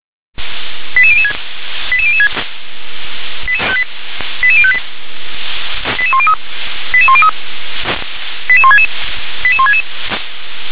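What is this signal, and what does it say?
Five-tone selective call (ZVEI), heard as quick runs of five tones, tones of about 70 ms. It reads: 88006 88006 88058 88103 88103 88166 88166
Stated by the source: ZVEI five-tone selective call received on SDR.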